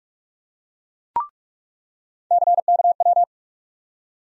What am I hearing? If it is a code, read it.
Morse code, heard as a keyed tone, about 46 wpm, 700 Hz. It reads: CKW
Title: USA Call Sign Postfixes - 3 Letters 45wpm